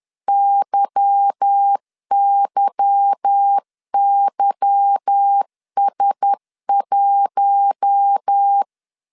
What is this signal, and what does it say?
Morse code, heard as a keyed tone, about 10 wpm, 800 Hz. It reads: YYYS1
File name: LEZIONI / 8 YJ45 / 8 Y 10wpm 2615